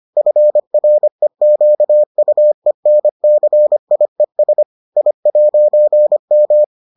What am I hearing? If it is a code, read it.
Morse code, heard as a keyed tone, about 25 wpm, 600 Hz. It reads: FREQUENCIES I'M